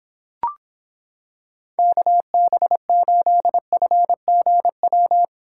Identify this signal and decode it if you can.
Morse code, heard as a keyed tone, about 26 wpm, 700 Hz. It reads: KB8FGW